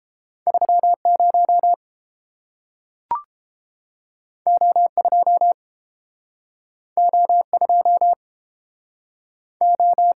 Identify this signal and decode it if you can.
Morse code, heard as a keyed tone, about 30 wpm, 700 Hz. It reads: 30 O2 O2 O